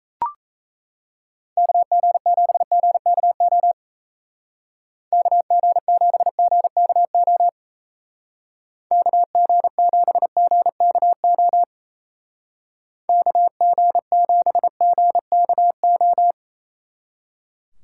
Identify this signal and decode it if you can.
Morse code, heard as a keyed tone, about 34 wpm, 700 Hz. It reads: KG7GKO KG7GKO KG7GKO KG7GKO
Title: Speed Racing - USA Call Signs 28wpm